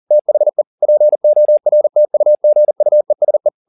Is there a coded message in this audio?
Yes: Morse code from 0.1 to 3.5 s, heard as a keyed tone, about 40 wpm, 600 Hz. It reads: THE PORTUGUESE